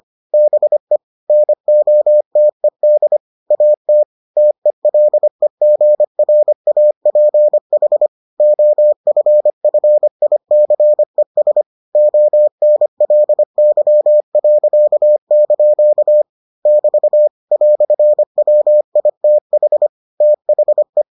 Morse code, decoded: BE NOTED AT TELEGRAPH OFFICES ONLY.) <BT> "WITH THE

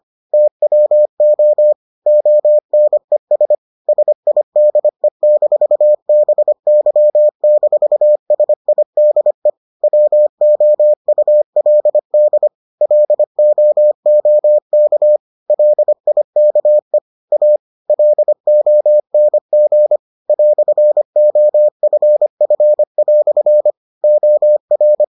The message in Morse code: TWO ONES SIDE-BY-SIDE WOULD LOOK LIKE A LONG "OFF" OR

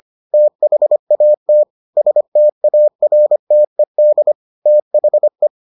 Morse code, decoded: THAT STARTED THE

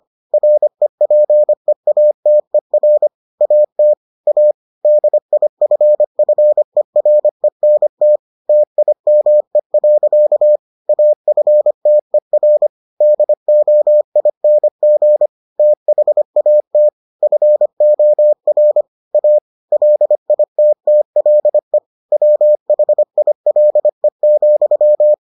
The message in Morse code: REPEATER AT A DIFFERENT TIME. AFTER DOING THAT FOR A LITTLE WHILE,

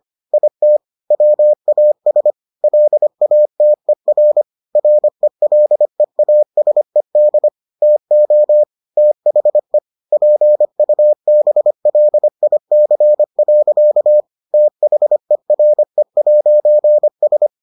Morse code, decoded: IT WAS LATER RELEASED TO THE PUBLIC. THERE'S

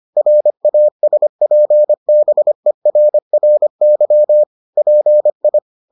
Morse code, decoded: RASPBERRY PI

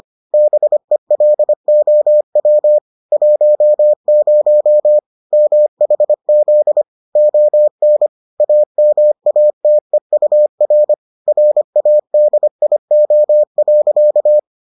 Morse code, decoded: BELOW 10 MHZ ON AMATEUR RADIO.

